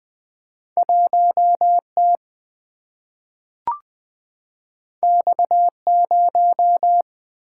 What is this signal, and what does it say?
Morse code, heard as a keyed tone, about 20 wpm, 700 Hz. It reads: 1T X0
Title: Single-Letter Single-Number 20wpm